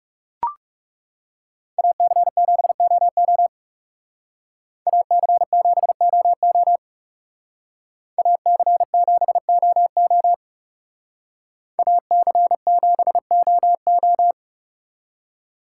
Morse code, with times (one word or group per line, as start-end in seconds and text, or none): 1.8-3.5 AC7OO
4.9-6.8 AC7OO
8.2-10.3 AC7OO
11.8-14.3 AC7OO